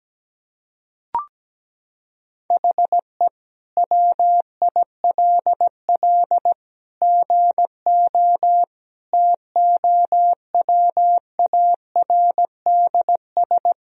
Morse code, heard as a keyed tone, about 17 wpm, 700 Hz. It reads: HE WILL GO TOWARDS
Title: Sentences from Top 1000 Words - Review Part 2 - 17wpm